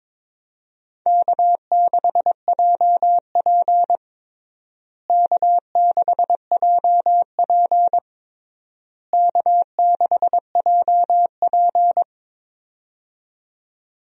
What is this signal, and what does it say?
Morse code, heard as a keyed tone, about 21 wpm, 700 Hz. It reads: K6JP K6JP K6JP
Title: USA Call Signs - 4 Characters - Repeated 3x - 22wpm